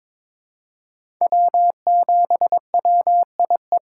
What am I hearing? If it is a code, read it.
Morse code, heard as a keyed tone, about 22 wpm, 700 Hz. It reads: W7WIE
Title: USA Call Signs - 5 Characters - 6x Character Spacing - 22wpm